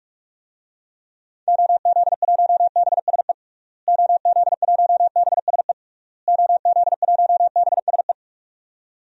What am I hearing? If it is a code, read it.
Morse code, heard as a keyed tone, about 46 wpm, 700 Hz. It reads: OZ1BSE OZ1BSE OZ1BSE
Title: DXCC-100 Call Signs - 6 Characters - Repeated 3x - 45wpm